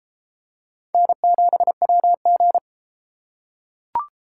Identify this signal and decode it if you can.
Morse code, heard as a keyed tone, about 33 wpm, 700 Hz. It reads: N7WG